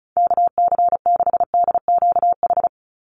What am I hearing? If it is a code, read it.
Morse code, heard as a keyed tone, about 34 wpm, 700 Hz. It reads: KC6DQH
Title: USA Call Signs 35wpm